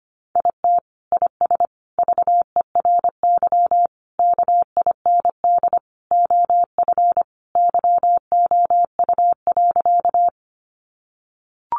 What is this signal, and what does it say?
Morse code, heard as a keyed tone, about 25 wpm, 700 Hz. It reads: IT IS VERY KIND OF YOU.
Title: Sentences from Top 200 Words 25wpm